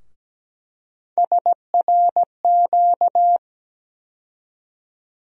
Morse code, heard as a keyed tone, about 17 wpm, 700 Hz. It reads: SRQ
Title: USA Call Sign Postfixes - 3 Letters 17wpm